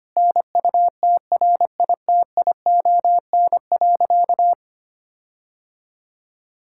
Morse code, heard as a keyed tone, about 25 wpm, 700 Hz. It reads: NUTRITION.
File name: practice / 25wpm / Difficult Sentences with Hints - 25wpm